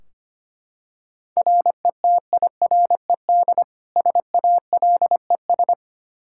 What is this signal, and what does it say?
Morse code, heard as a keyed tone, about 25 wpm, 700 Hz. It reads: RETIRED SALES
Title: Practice QSO - Exchange 3 - More Info - 25wpm